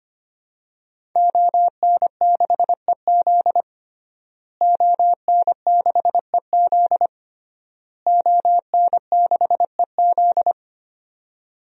Morse code, heard as a keyed tone, about 25 wpm, 700 Hz. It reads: ON6EZ ON6EZ ON6EZ
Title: DXCC-100 Call Signs - 5 Characters - Repeated 3x - 25wpm